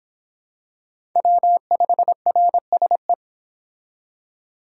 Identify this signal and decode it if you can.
Morse code, heard as a keyed tone, about 25 wpm, 700 Hz. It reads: W5RSE